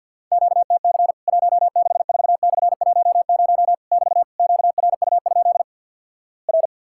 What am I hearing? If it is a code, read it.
Morse code, heard as a keyed tone, about 50 wpm, 700 Hz. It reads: QTC 164/10 <BT> QRU? R
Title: Worked All Europe DX Contest - QTC Exchange with Repeats - 50wpm